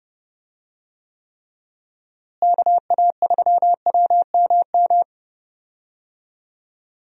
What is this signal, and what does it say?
Morse code, heard as a keyed tone, about 30 wpm, 700 Hz. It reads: KA3WMM
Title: USA Call Signs - 6 Characters - Repeated 2x - 30wpm